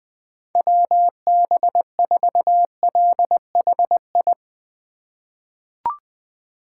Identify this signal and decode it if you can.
Morse code, heard as a keyed tone, about 21 wpm, 700 Hz. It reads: WB4LHI
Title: USA Call Signs - 6 Characters - 3x Character Spacing - 20wpm